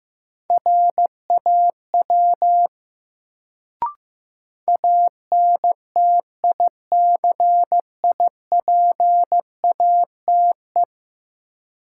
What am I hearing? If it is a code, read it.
Morse code, heard as a keyed tone, about 15 wpm, 700 Hz. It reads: RAW ANTICIPATE
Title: Sentences from Top 1500 Words - New Word Drill - 15wpm